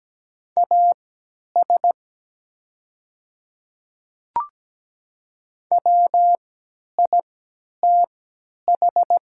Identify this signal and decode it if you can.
Morse code, heard as a keyed tone, about 17 wpm, spaced as for 10 wpm, 700 Hz. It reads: AS WITH
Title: 3x Character Spacing - Top 100 Words - 17wpm